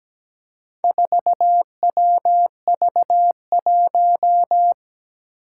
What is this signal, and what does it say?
Morse code, heard as a keyed tone, about 17 wpm, 700 Hz. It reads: 4WV1
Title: Four Challenging Letter-Number Combinations 17wpm